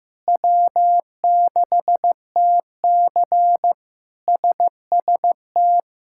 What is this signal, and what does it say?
Morse code, heard as a keyed tone, about 16 wpm, 700 Hz. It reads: W6TC SST
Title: K1USN SST QSO Practice - 15wpm